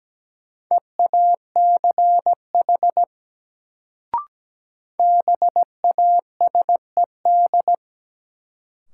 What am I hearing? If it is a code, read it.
Morse code, heard as a keyed tone, about 17 wpm, 700 Hz. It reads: EACH BASED
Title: Sets of 2 Words - New Word Drill - 17wpm